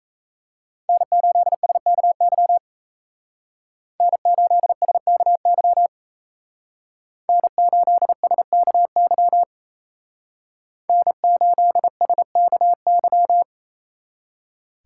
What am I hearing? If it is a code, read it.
Morse code, heard as a keyed tone, about 34 wpm, 700 Hz. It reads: N8SKY N8SKY N8SKY N8SKY
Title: Speed Racing - USA Call Signs 28wpm